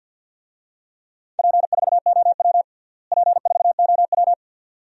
Morse code, heard as a keyed tone, about 51 wpm, 700 Hz. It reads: P4OW P4OW
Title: DXCC-100 Call Signs - 4 Characters - Repeated 2x - 50wpm